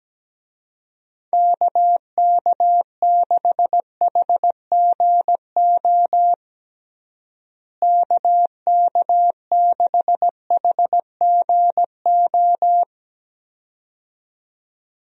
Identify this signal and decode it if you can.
Morse code, heard as a keyed tone, about 17 wpm, 700 Hz. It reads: KK6HGO KK6HGO
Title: USA Call Signs - 6 Characters - Repeated 2x - 17wpm